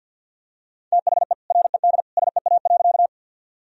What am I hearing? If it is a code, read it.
Morse code, heard as a keyed tone, about 50 wpm, 700 Hz. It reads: THE RED SEA.
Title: Sentences from Top 1000 Words - Review Part 2 - 50wpm